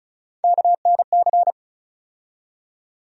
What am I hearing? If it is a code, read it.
Morse code, heard as a keyed tone, about 36 wpm, 700 Hz. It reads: KNC
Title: USA Call Sign Postfixes - 3 Letters 35wpm